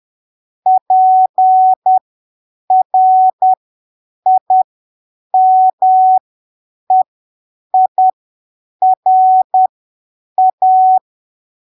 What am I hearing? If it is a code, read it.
Morse code, heard as a keyed tone, about 10 wpm, spaced as for 7 wpm, 750 Hz. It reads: PRIMEIRA
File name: arquivos / texto / texto 29 10ppm